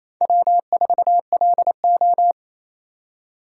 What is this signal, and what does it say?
Morse code, heard as a keyed tone, about 32 wpm, 700 Hz. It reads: W4LO
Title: Speed Racing - USA 1x2 Call Signs 28wpm